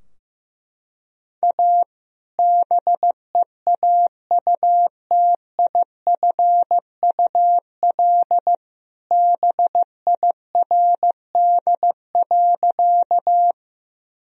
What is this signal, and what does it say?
Morse code, heard as a keyed tone, about 15 wpm, 700 Hz. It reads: A BEAUTIFUL BIRD.
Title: Sentences from Top 1000 Words - Review Part 2 - 15wpm